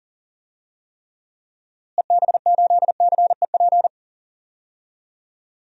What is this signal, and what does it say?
Morse code, heard as a keyed tone, about 40 wpm, 700 Hz. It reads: EB8CEP
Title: DXCC-100 Call Signs - 6 Characters - Repeated 3x - 40wpm